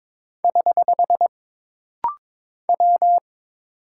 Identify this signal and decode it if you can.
Morse code, heard as a keyed tone, about 22 wpm, 700 Hz. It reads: <HH> W
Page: This rapid-fire timing is geared to push your ICR (Instant Character Recognition) skills.